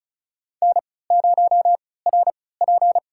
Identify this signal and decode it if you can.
Morse code, heard as a keyed tone, about 36 wpm, spaced as for 20 wpm, 700 Hz. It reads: N0RP
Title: USA Call Signs - 4 Characters - 3x Character Spacing - 35wpm